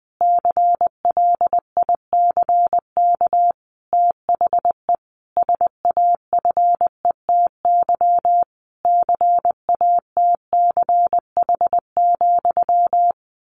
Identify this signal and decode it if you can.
Morse code, heard as a keyed tone, about 19 wpm, 700 Hz. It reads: CLICK THE SAFETY CATCH,
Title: A Sound Of Thunder 20wpm